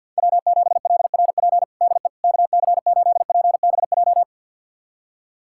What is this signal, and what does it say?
Morse code, heard as a keyed tone, about 50 wpm, 690 Hz. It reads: W7LRP DE KC8PBJ
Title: USA Callsign DE Callsign 50wpm